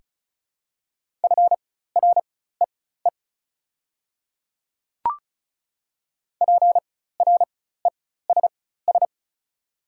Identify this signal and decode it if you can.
Morse code, heard as a keyed tone, about 34 wpm, spaced as for 16 wpm, 690 Hz. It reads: FREE PRESS
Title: 4x Character Spacing - Top 500 Words - 35wpm